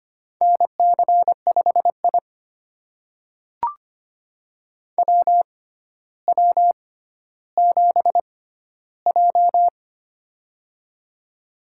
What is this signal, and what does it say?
Morse code, heard as a keyed tone, about 25 wpm, spaced as for 13 wpm, 700 Hz. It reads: NC5I WW7J
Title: USA Call Signs - 4 Characters - 6x Character Spacing - 25wpm